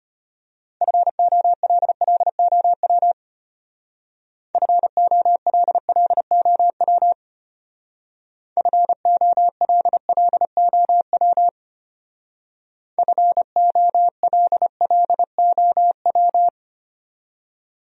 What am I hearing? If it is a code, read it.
Morse code, heard as a keyed tone, about 32 wpm, 700 Hz. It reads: FOLLOW FOLLOW FOLLOW FOLLOW